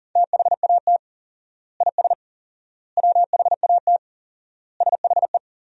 Morse code, heard as a keyed tone, about 37 wpm, 700 Hz. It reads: THAT IS WHAT SHE
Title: 4x Word-Spacing - Sentences from Top 100 Words - 40wpm